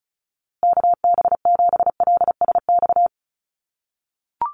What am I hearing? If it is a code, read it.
Morse code, heard as a keyed tone, about 34 wpm, 700 Hz. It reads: KB7LSX
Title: USA Call Signs Repeated 2x 35wpm